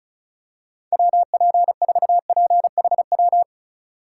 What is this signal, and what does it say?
Morse code, heard as a keyed tone, about 36 wpm, 700 Hz. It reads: WP4PHW
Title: USA Call Signs - 6 Characters - 2x Character Spacing - 35wpm